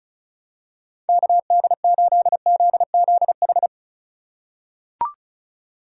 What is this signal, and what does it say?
Morse code, heard as a keyed tone, about 36 wpm, 700 Hz. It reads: KD8ZZH
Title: USA Call Signs - 6 Characters - 3x Character Spacing - 35wpm